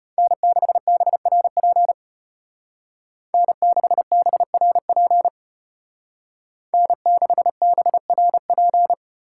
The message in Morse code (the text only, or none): N6BRP N6BRP N6BRP